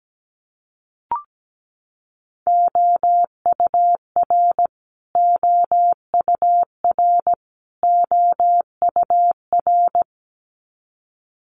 Morse code, heard as a keyed tone, about 17 wpm, 700 Hz. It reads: OUR OUR OUR
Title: Top 500 Words Repeated 17wpm